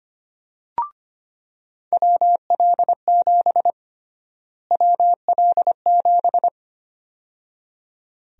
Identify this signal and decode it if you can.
Morse code, heard as a keyed tone, about 25 wpm, 700 Hz. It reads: WL7 WL7
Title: USA Call Sign Prefixes - Repeated 2x - 25wpm